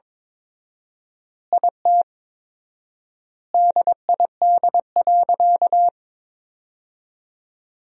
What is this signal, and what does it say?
Morse code, heard as a keyed tone, about 22 wpm, 700 Hz. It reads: IT DID.
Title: 4x Word-Spacing - Sentences from Top 500 Words - 22wpm